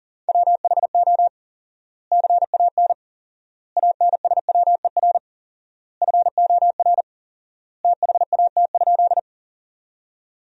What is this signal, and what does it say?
Morse code, heard as a keyed tone, about 37 wpm, 700 Hz. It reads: WHO CAN ANSWER FOR THAT?
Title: 4x Word-Spacing - Sentences from Top 300 Words - 40wpm